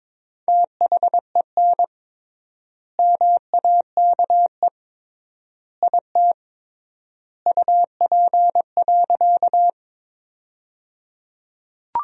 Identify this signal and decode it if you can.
Morse code, heard as a keyed tone, about 21 wpm, 700 Hz. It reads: THEN MAKE IT UP.
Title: 3x Word-Spacing - Sentences from Top 100 Words - 22wpm